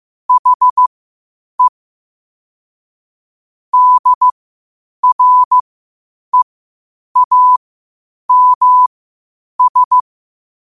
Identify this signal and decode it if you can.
Morse code, heard as a keyed tone, about 15 wpm, spaced as for 8 wpm, 1000 Hz. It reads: HE DREAMS
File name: qotdfiles / 8wpm